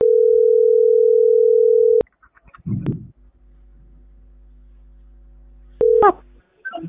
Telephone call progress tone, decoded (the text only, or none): ringback tone